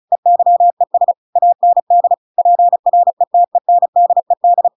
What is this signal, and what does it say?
Morse code, heard as a keyed tone, about 34 wpm, 700 Hz. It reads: EYES AND PRETENDED
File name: Pinocchio / Pinocchio ch8 35 35 ik1hge